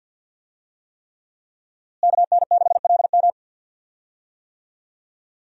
Morse code, heard as a keyed tone, about 53 wpm, 700 Hz. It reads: KN6LM
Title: USA Call Signs - 5 Characters - Repeated 3x - 50wpm